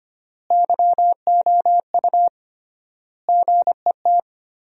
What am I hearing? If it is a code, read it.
Morse code, heard as a keyed tone, about 25 wpm, 700 Hz. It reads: YOU GET
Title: 3x Word-Spacing - Sentences from Top 200 Words - 25wpm